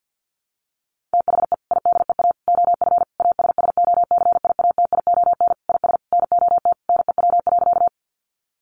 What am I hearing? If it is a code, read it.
Morse code, heard as a keyed tone, about 50 wpm, 700 Hz. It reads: THE IDEA OF ASSOCIATION IS NOT NEW.
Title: Sentences from Top 1100 Words 50wpm